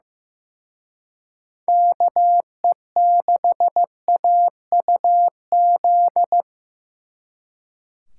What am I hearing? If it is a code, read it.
Morse code, heard as a keyed tone, about 18 wpm, 700 Hz. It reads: KE6AUZ
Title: Speed Racing - USA 2x3 Call Signs 15wpm